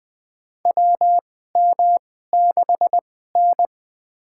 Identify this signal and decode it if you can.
Morse code, heard as a keyed tone, about 20 wpm, spaced as for 14 wpm, 700 Hz. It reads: WM6N
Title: USA Call Signs - 2x Character Spacing - 20wpm